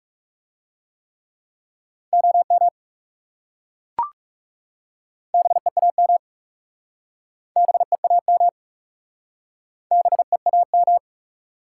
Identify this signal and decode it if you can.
Morse code, heard as a keyed tone, about 39 wpm, 700 Hz. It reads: OM BEAM BEAM BEAM